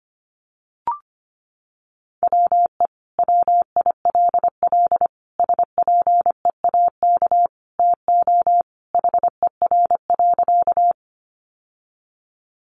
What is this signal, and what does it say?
Morse code, heard as a keyed tone, about 25 wpm, 700 Hz. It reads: WE WILL SPEAK TO HER.
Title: Sentences from Top 1100 Words 25wpm